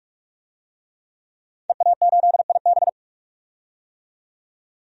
Morse code, heard as a keyed tone, about 44 wpm, 700 Hz. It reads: EA8IB
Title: DXCC-100 Call Signs - 5 Characters - 45wpm